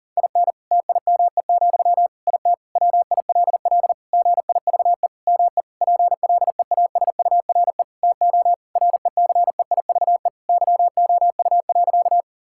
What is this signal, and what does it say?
Morse code, heard as a keyed tone, about 40 wpm, 700 Hz. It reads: IN TIME, IT WILL GIVE ME PLEASURE TO RECEIVE YOU.
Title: Sentences from Top 1600 Words 40wpm